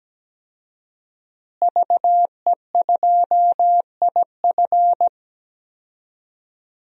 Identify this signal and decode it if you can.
Morse code, heard as a keyed tone, about 17 wpm, 700 Hz. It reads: VE2IF
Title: DXCC-100 Call Signs - 5 Characters - Repeated 3x - 17wpm